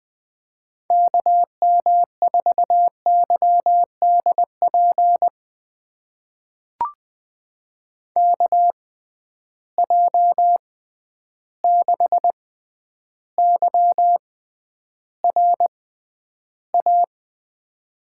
Morse code, heard as a keyed tone, about 20 wpm, spaced as for 10 wpm, 700 Hz. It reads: KM4YDP KJ6YRA
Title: USA Call Signs - 6 Characters - 6x Character Spacing - 20wpm